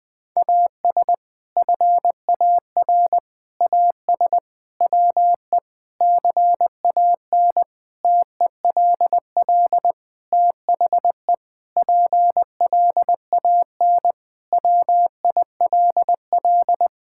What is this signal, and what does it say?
Morse code, heard as a keyed tone, about 20 wpm, 700 Hz. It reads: AS FAR AS WE CAN TELL THE PLAN WILL
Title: Sentences with 6-Word Phrases - 20wpm